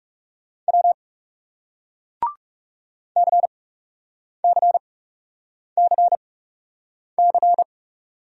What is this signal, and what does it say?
Morse code, heard as a keyed tone, about 40 wpm, spaced as for 8 wpm, 700 Hz. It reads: W CCCC